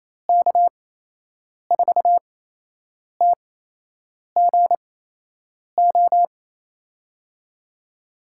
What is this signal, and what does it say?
Morse code, heard as a keyed tone, about 28 wpm, spaced as for 8 wpm, 700 Hz. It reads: K4TGO
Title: USA Call Signs - 6 Characters - 8x Character Spacing - 28wpm